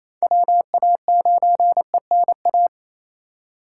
Morse code, heard as a keyed tone, about 28 wpm, 700 Hz. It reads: WA9ENA
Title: K1USN SST QSO Practice - 28wpm